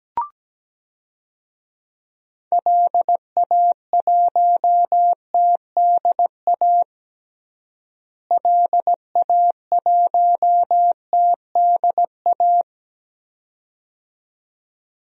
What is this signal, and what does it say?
Morse code, heard as a keyed tone, about 17 wpm, 700 Hz. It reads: LA1TDA LA1TDA
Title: DXCC-100 Call Signs - 6 Characters - Repeated 2x - 17wpm